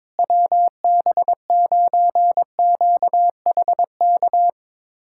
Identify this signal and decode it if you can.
Morse code, heard as a keyed tone, about 22 wpm, 700 Hz. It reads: WB9QHK
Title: USA Callsign DE Callsign 22wpm